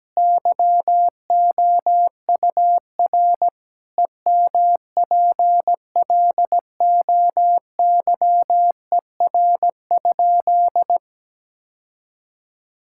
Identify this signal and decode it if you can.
Morse code, heard as a keyed tone, about 17 wpm, 700 Hz. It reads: YOUR EMPLOYER?
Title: Sentences from Top 1600 Words 17wpm